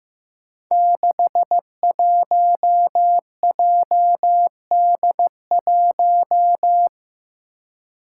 Morse code, heard as a keyed tone, about 15 wpm, 700 Hz. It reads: 61JD1
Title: Five Challenging Letter-Number Combinations 15wpm